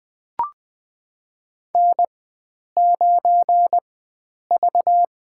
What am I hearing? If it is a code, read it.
Morse code, heard as a keyed tone, about 20 wpm, spaced as for 9 wpm, 700 Hz. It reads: N9V
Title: USA Call Signs - 4x Character Spacing - 20wpm